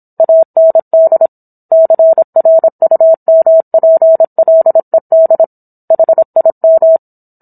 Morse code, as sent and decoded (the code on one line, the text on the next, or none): .- -. -.. / -.-. .-. ..- -- .--. .-.. -.. / .... .. --
AND CRUMPLED HIM